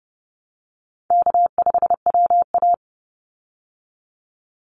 Morse code, heard as a keyed tone, about 30 wpm, 700 Hz. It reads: K5WA
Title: Call Signs Repeated 2x - 30wpm